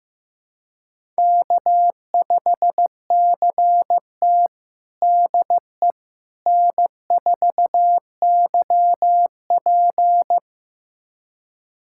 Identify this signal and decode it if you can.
Morse code, heard as a keyed tone, about 15 wpm, 700 Hz. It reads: K5CT DE N4YP
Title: USA Callsign DE Callsign - 4 Characters - 15wpm